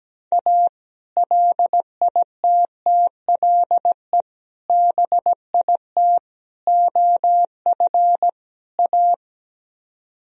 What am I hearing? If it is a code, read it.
Morse code, heard as a keyed tone, about 17 wpm, 700 Hz. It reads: A LITTLE BIT OF A
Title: Speed-Racing - Sets of 5 Words 17wpm